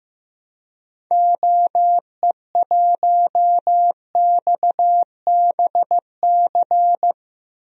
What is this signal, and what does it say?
Morse code, heard as a keyed tone, about 16 wpm, 700 Hz. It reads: OE1XBC
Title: DXCC-100 Call Signs - 6 Characters - 15wpm